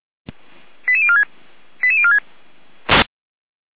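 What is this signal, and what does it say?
Five-tone selective call (ZVEI), heard as quick runs of five tones, tones of about 80 ms. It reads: 88046 88046
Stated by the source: ZVEI five-tone selective call 88169 received via SDR, from the Feuerwehr Orlamünde (feuerwehr-orlamuende.org) audio server.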